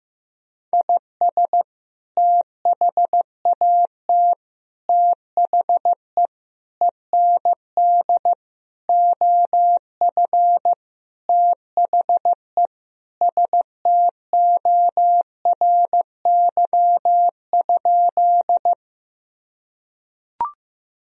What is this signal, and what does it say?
Morse code, heard as a keyed tone, about 16 wpm, 700 Hz. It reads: IS THAT THE END OF THE STORY?
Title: Sentences from Top 1000 Words - Review Part 1 - 15wpm